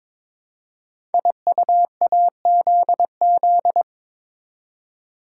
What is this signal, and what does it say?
Morse code, heard as a keyed tone, about 22 wpm, 700 Hz. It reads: IUAZZ
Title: Code Groups - 5 Letters - 22wpm